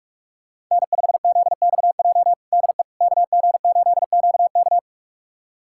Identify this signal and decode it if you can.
Morse code, heard as a keyed tone, about 46 wpm, 700 Hz. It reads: N5ZXJ DE KG8QK
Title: USA Callsign DE Callsign - 5 Characters - 45wpm